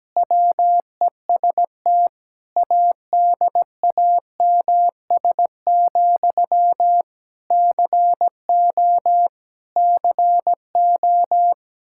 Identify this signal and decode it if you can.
Morse code, heard as a keyed tone, about 17 wpm, 700 Hz. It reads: WEST ADAMS, CO CO
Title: Practice QSO - Exchange 1 - Beginning - 17wpm